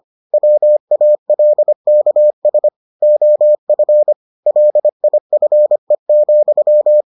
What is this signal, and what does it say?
Morse code, heard as a keyed tone, about 25 wpm, 600 Hz. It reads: WALKS OF LIFE,